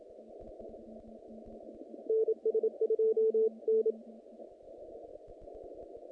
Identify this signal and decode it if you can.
Morse code, heard as a keyed tone, about 26 wpm, 440 Hz. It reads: NS2N
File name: contests / naqp-cw-k5tr-op-k5pi- -2300